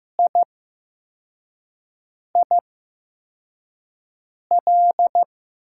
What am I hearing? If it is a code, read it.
Morse code, heard as a keyed tone, about 15 wpm, spaced as for 4 wpm, 700 Hz. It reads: IIL
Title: USA Call Signs - 5 Characters - 8x Character Spacing - 15wpm